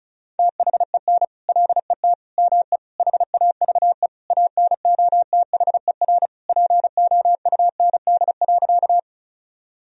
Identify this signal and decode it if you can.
Morse code, heard as a keyed tone, about 36 wpm, 700 Hz. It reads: THEN LET ME HAVE ANOTHER POUND.
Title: Sentences from Top 1000 Words - Review Part 2 - 35wpm